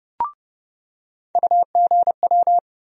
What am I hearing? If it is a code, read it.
Morse code, heard as a keyed tone, about 30 wpm, 700 Hz. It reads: UGW
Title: USA Call Sign Postfixes - 3 Letters 30wpm